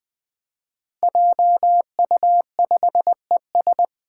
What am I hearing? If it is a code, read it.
Morse code, heard as a keyed tone, about 20 wpm, 700 Hz. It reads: JU5ES
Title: Five Challenging Letter-Number Combinations 20wpm